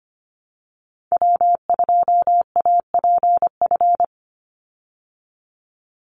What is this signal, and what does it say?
Morse code, heard as a keyed tone, about 25 wpm, 700 Hz. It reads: W2APF
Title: Call Signs Repeated 3x - 25wpm